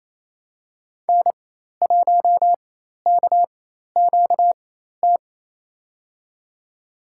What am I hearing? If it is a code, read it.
Morse code, heard as a keyed tone, about 28 wpm, spaced as for 13 wpm, 700 Hz. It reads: N1KQT
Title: USA Call Signs - 5 Characters - 4x Character Spacing - 28wpm